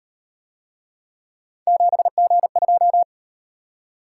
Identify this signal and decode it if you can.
Morse code, heard as a keyed tone, about 38 wpm, 700 Hz. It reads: 7G2